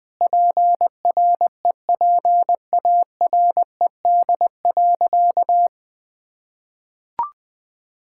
Morse code, heard as a keyed tone, about 20 wpm, 700 Hz. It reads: PREPARED.
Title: Sentences with 6-Word Phrases - Spoken Beginning - 20wpm